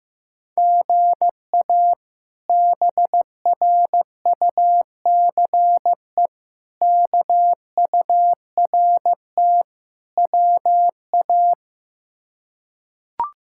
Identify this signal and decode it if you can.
Morse code, heard as a keyed tone, about 15 wpm, 700 Hz. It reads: GA BRUCE KURT WA
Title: K1USN SST QSO Practice - 15wpm